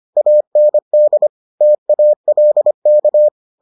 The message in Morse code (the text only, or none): AND TALK